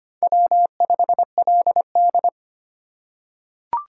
Morse code, heard as a keyed tone, about 25 wpm, 700 Hz. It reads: W5LD
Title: USA Call Signs - 4 Characters - 3x Character Spacing - 25wpm